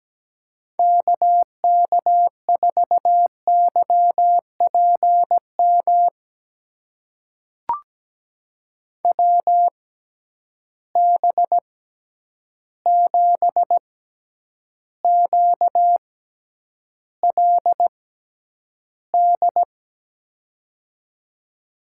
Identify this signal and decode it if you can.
Morse code, heard as a keyed tone, about 17 wpm, spaced as for 9 wpm, 700 Hz. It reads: KK4YPM WB7QLD
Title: USA Call Signs - 6 Characters - 6x Character Spacing - 17wpm